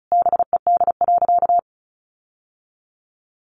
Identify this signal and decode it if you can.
Morse code, heard as a keyed tone, about 34 wpm, 700 Hz. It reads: BED.
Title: Sentences from Top 900 Words 35wpm